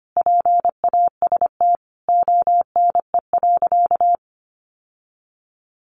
Morse code, heard as a keyed tone, about 25 wpm, 700 Hz. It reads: PAST ONE.
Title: Sentences from Top 1300 Words 25wpm